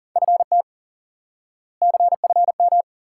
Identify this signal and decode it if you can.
Morse code, heard as a keyed tone, about 40 wpm, 700 Hz. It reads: FT CFM
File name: course / instant-qso-element-course-3 / IQE Course 3 - Lesson 89 - YAGI - Two Elements - 40wpm